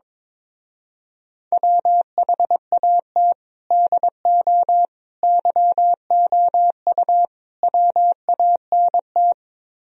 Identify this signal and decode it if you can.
Morse code, heard as a keyed tone, about 23 wpm, 700 Hz. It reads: WHAT DO YOU WANT
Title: Speed-Racing - Sets of 4 Words 22wpm